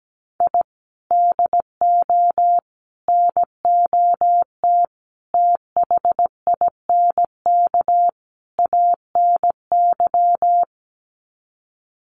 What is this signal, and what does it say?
Morse code, heard as a keyed tone, about 17 wpm, 700 Hz. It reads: I DO NOT THINK ANY
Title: Sets of 5 Words Encore 17wpm